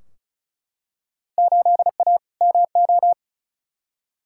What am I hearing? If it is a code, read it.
Morse code, heard as a keyed tone, about 36 wpm, 700 Hz. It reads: 8A MO
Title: Field Day Exchanges Encore - 35wpm